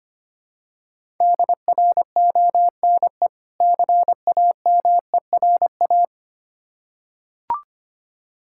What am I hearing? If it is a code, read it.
Morse code, heard as a keyed tone, about 25 wpm, 690 Hz. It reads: DRONE CAMERA
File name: practice / 25wpm / Difficult Words with Hints - 25wpm